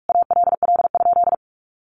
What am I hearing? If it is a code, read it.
Morse code, heard as a keyed tone, about 44 wpm, 700 Hz. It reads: ALL?
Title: Sentences from Top 1400 Words 45wpm